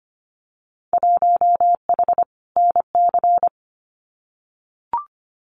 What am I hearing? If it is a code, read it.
Morse code, heard as a keyed tone, about 25 wpm, 700 Hz. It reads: 1H NC
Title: Winter Field Day Exchanges 25wpm